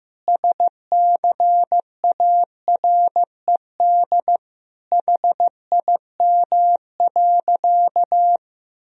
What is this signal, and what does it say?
Morse code, heard as a keyed tone, about 15 wpm, 700 Hz. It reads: SCARED HIM.
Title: Sentences from Top 1000 Words - Review Part 4 - 15wpm